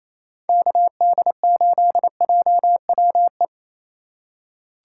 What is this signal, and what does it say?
Morse code, heard as a keyed tone, about 27 wpm, 700 Hz. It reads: KD8JWE
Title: USA Call Signs - 2x Character Spacing - 28wpm